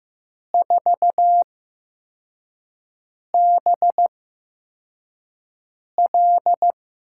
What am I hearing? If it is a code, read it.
Morse code, heard as a keyed tone, about 15 wpm, spaced as for 4 wpm, 700 Hz. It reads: 4BL
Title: USA Call Signs - 8x Character Spacing - 15wpm